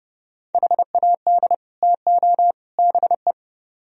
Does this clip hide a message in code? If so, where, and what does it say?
Yes: Morse code from 0.5 to 3.3 s, heard as a keyed tone, about 29 wpm, 700 Hz. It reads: HAD TO BE